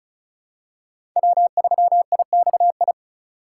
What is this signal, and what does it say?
Morse code, heard as a keyed tone, about 36 wpm, 700 Hz. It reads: W3IXI
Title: USA Call Signs - 5 Characters - 4x Character Spacing - 35wpm